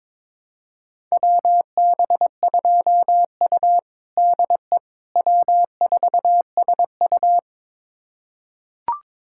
Morse code, heard as a keyed tone, about 22 wpm, 700 Hz. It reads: WB2U DE W4SU
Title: USA Callsign DE Callsign - 4 Characters - 22wpm